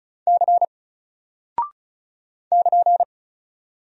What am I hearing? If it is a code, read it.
Morse code, heard as a keyed tone, about 35 wpm, 700 Hz. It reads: C <KN>
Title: Single Character (Letter:Number:Punctuation:Prosign) Alt 35wpm